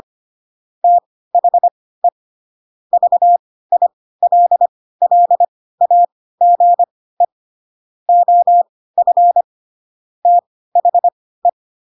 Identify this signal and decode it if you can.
Morse code, heard as a keyed tone, about 25 wpm, spaced as for 16 wpm, 700 Hz. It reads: THE VILLAGE OF THE
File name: Pinocchio / Pinocchio ch6 10 25 ik1hge